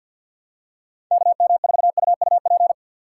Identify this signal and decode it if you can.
Morse code, heard as a keyed tone, about 53 wpm, 700 Hz. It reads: KM4UUP
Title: USA Call Signs - 6 Characters - 4x Character Spacing - 50wpm